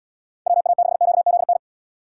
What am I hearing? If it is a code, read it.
Morse code, heard as a keyed tone, about 71 wpm, 700 Hz. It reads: WA6QZN